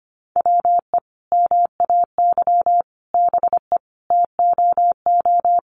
Morse code, heard as a keyed tone, about 25 wpm, 700 Hz. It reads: WE MAY BE TOO
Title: Sentences from Top 300 Words 25wpm